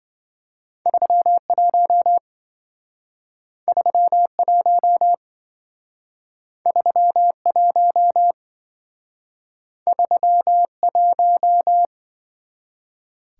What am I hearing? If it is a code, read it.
Morse code, heard as a keyed tone, about 26 wpm, 700 Hz. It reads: 31 31 31 31